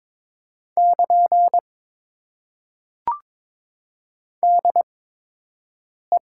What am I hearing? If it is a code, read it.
Morse code, heard as a keyed tone, about 22 wpm, spaced as for 6 wpm, 700 Hz. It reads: <KN> DE
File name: Top 90 QSO Elements - 8x Character Spacing - 22wpm